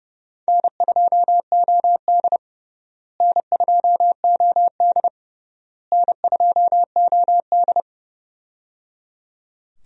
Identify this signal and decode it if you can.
Morse code, heard as a keyed tone, about 29 wpm, 700 Hz. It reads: N2OD N2OD N2OD
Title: USA Call Signs - 4 Characters - Repeated 3x - 30wpm